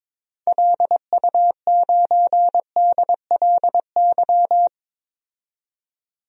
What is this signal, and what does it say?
Morse code, heard as a keyed tone, about 22 wpm, 700 Hz. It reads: LU9DLY
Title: DXCC-100 Call Signs - 6 Characters - 22wpm